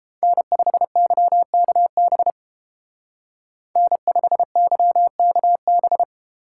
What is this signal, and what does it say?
Morse code, heard as a keyed tone, about 30 wpm, 700 Hz. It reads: N5YKB N5YKB